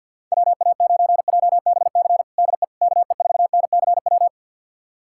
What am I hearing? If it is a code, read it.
Morse code, heard as a keyed tone, about 52 wpm, 700 Hz. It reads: WA9JBC DE KE4NCW
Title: USA Callsign DE Callsign - 6 Characters - 50wpm